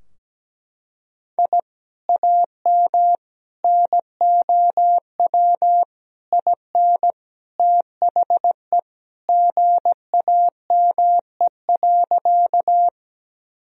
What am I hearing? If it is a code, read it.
Morse code, heard as a keyed tone, about 17 wpm, 700 Hz. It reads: I AM NOW IN THE GAME.
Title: Sentences from Top 1000 Words - Review Part 3 - 17wpm